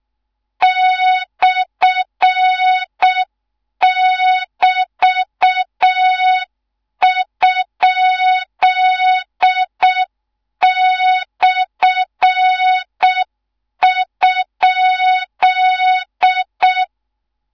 Morse code, heard as a keyed tone, about 6 wpm, 3700 Hz, with noise. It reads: /<BT>?/?
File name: Morze-HA2MN / hang / 13IRASJELEK1 HA2MN